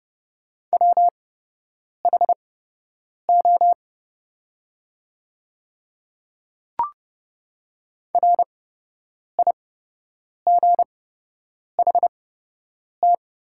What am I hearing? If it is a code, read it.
Morse code, heard as a keyed tone, about 30 wpm, spaced as for 8 wpm, 700 Hz. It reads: WHO RIGHT
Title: 8x Character Spacing - Top 200 Words - 30wpm